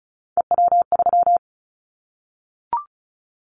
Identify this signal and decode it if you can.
Morse code, heard as a keyed tone, about 36 wpm, 700 Hz. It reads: EW3
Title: Callsign Prefixes for Top 100 DXCC - 35wpm